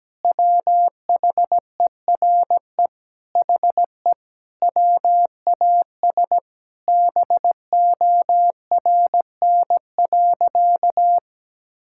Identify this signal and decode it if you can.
Morse code, heard as a keyed tone, about 17 wpm, 700 Hz. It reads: WHERE HE WAS BORN.
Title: Sentences from Top 1600 Words 17wpm